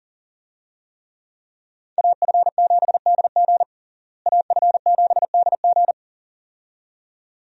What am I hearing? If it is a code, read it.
Morse code, heard as a keyed tone, about 40 wpm, 700 Hz. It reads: AF7DG AF7DG
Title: USA Call Signs - 5 Characters - Repeated 2x - 40wpm